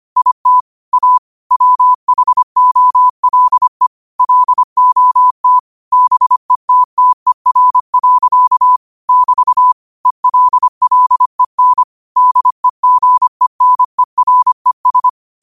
Morse code, heard as a keyed tone, about 25 wpm, 1000 Hz. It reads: IT A WHOLE LOT BETTER. <BT> ELLEN DEGENERES